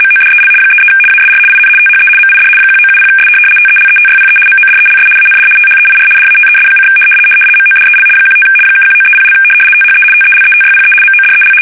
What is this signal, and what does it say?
RTTY, about 75 baud. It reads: MODEL 33 ASR IN THE MID 1970S TELETYPE MODELS AND THEIR DATES: MODEL 15 (1930) THE TELETYPE MODEL
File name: fsk850-75B / teletype-corp